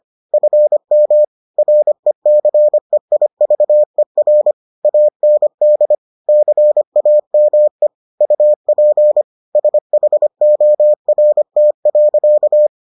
Morse code, heard as a keyed tone, about 25 wpm, 600 Hz. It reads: FM RECEIVER AND CAME UP SHORT.